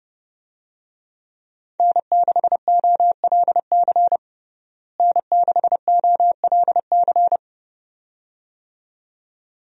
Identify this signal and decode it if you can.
Morse code, heard as a keyed tone, about 30 wpm, 700 Hz. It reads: N6OLC N6OLC
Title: USA Call Signs - 5 Characters - Repeated 2x - 30wpm